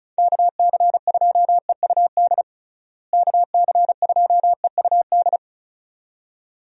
Morse code, heard as a keyed tone, about 36 wpm, 700 Hz. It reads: KC2EUD KC2EUD
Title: USA Call Signs - 6 Characters - Repeated 2x - 35wpm